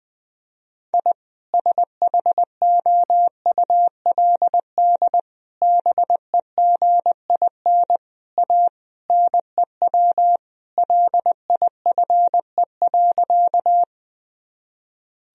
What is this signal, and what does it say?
Morse code, heard as a keyed tone, about 21 wpm, 700 Hz. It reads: I SHOULD BEGIN A NEW LIFE.
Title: Sentences from Top 1000 Words - Review Part 1 - 20wpm